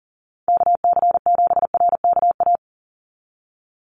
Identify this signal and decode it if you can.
Morse code, heard as a keyed tone, about 40 wpm, 700 Hz. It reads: KC7RKA
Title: USA Call Signs 40wpm